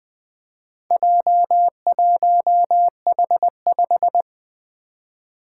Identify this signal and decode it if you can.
Morse code, heard as a keyed tone, about 20 wpm, 700 Hz. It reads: J1H5
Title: Four Letter-Number Combinations - 20wpm